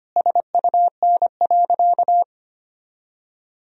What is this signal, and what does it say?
Morse code, heard as a keyed tone, about 25 wpm, 700 Hz. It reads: SUN.
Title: 4x Word-Spacing - Sentences from Top 300 Words - 25wpm